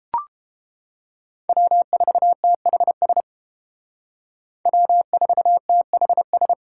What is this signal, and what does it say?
Morse code, heard as a keyed tone, about 31 wpm, 700 Hz. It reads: W4THS W4THS